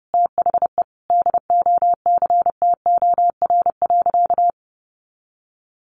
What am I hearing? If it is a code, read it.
Morse code, heard as a keyed tone, about 30 wpm, 700 Hz. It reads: THE DOCTOR.
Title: Sentences from Top 1500 Words 30wpm